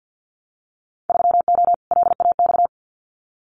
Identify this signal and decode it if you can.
Morse code, heard as a keyed tone, about 50 wpm, 700 Hz. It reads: 3O LAX
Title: Winter Field Day Exchanges Encore 50wpm